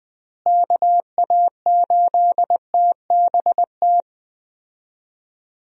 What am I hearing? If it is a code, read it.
Morse code, heard as a keyed tone, about 20 wpm, 700 Hz. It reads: KA8TBT
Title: USA Call Signs - 6 Characters - Repeated 2x - 20wpm